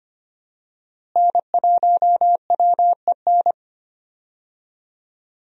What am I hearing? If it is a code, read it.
Morse code, heard as a keyed tone, about 25 wpm, 700 Hz. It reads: N1WEN
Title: USA Call Signs - 5 Characters - Repeated 2x - 25wpm